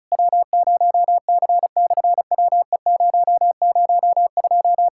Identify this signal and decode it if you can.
Morse code, heard as a keyed tone, about 33 wpm, 700 Hz. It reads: W0C/WE002
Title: SOTA QSO Practice 35wpm